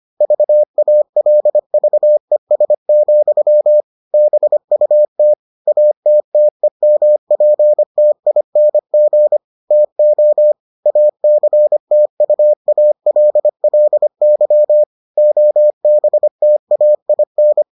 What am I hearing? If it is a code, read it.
Morse code, heard as a keyed tone, about 25 wpm, 600 Hz. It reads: VALVES, BUT ATTEMPTING TO ACTUALLY OBTAIN